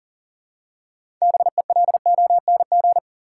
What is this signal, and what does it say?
Morse code, heard as a keyed tone, about 40 wpm, 700 Hz. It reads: BELONG
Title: Six Letter Words - 40wpm